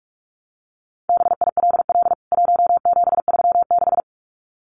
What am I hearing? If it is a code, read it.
Morse code, heard as a keyed tone, about 44 wpm, 700 Hz. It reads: BILL 1736
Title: CWT Exchange 45wpm